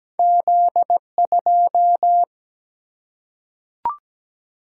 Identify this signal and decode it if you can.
Morse code, heard as a keyed tone, about 17 wpm, 700 Hz. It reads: Z2
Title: Single-Letter Single-Number 17wpm